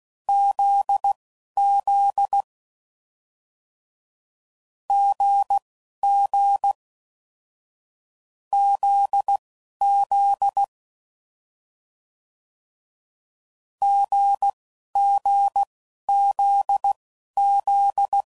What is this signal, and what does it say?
Morse code, heard as a keyed tone, about 16 wpm, spaced as for 11 wpm, 800 Hz. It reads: ZZ GG ZZ GGZZ